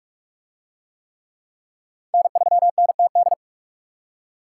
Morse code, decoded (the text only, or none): N3NTD